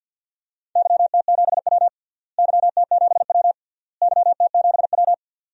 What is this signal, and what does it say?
Morse code, heard as a keyed tone, about 51 wpm, 700 Hz. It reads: YT7W YT7W YT7W
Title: DXCC-100 Call Signs - 4 Characters - Repeated 3x - 50wpm